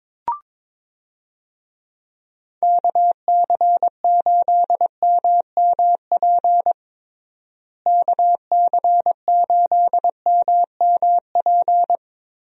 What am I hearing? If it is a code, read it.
Morse code, heard as a keyed tone, about 21 wpm, 700 Hz. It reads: KC8MMP KC8MMP
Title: USA Call Signs - 6 Characters - Repeated 3x - 22wpm